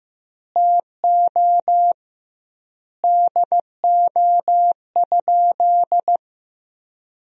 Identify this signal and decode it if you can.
Morse code, heard as a keyed tone, about 15 wpm, 700 Hz. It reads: TO DO?
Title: 2x Word-Spacing - Sentences from Top 200 Words - 15wpm